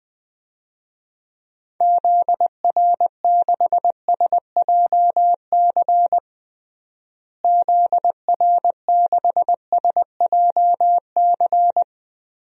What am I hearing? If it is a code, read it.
Morse code, heard as a keyed tone, about 20 wpm, 700 Hz. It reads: ZR6SJC ZR6SJC
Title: DXCC-100 Call Signs - 6 Characters - Repeated 3x - 20wpm